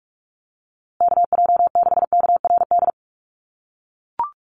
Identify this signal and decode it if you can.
Morse code, heard as a keyed tone, about 44 wpm, 700 Hz. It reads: KJ6KRD
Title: USA Call Signs Repeated 2x 45wpm